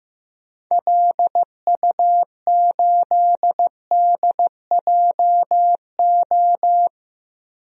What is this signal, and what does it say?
Morse code, heard as a keyed tone, about 15 wpm, 700 Hz. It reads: LU8DJO
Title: DXCC-100 Call Signs - 6 Characters - Repeated 3x - 15wpm